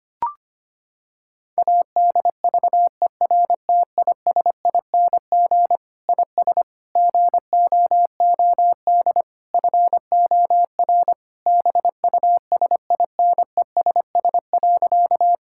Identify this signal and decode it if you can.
Morse code, heard as a keyed tone, about 25 wpm, 700 Hz. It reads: ADVERTISING IS GOOD FOR BUSINESS.
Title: Sentences from Top 1000 Words - Review Part 6 - 25wpm